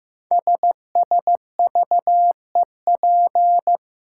Morse code, heard as a keyed tone, about 15 wpm, 700 Hz. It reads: SSVEP
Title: Code Groups - 5 Letters - 15wpm